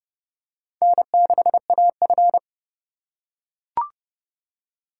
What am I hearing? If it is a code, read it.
Morse code, heard as a keyed tone, about 29 wpm, 700 Hz. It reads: N6AF